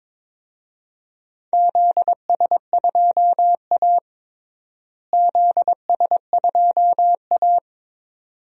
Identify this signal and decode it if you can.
Morse code, heard as a keyed tone, about 22 wpm, 700 Hz. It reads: ZS2A ZS2A
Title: DXCC-100 Call Signs - 4 Characters - Repeated 2x - 22wpm